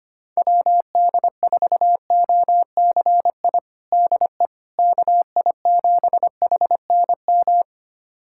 Morse code, heard as a keyed tone, about 25 wpm, 700 Hz. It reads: WD4OCI DE KI7HNM
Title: USA Callsign DE Callsign - 6 Characters - 25wpm